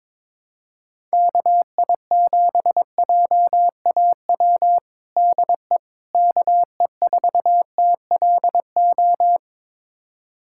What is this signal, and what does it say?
Morse code, heard as a keyed tone, about 22 wpm, 700 Hz. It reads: KI7JAW DE KE4TLO
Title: USA Callsign DE Callsign 22wpm